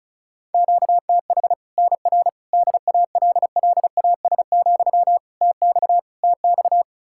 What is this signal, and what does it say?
Morse code, heard as a keyed tone, about 34 wpm, 700 Hz. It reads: QTH NR DALLAS, TX TX
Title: Practice QSO - Exchange 1 - Beginning - 35wpm